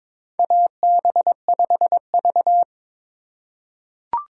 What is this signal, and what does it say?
Morse code, heard as a keyed tone, about 22 wpm, 700 Hz. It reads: AB5V
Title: USA Call Signs - 4 Characters - 3x Character Spacing - 22wpm